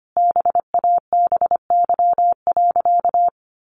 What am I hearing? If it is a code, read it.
Morse code, heard as a keyed tone, about 25 wpm, 700 Hz. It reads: BABY.
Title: Sentences from Top 1300 Words 25wpm